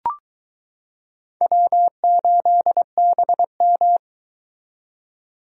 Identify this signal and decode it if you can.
Morse code, heard as a keyed tone, about 22 wpm, 700 Hz. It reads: W8BM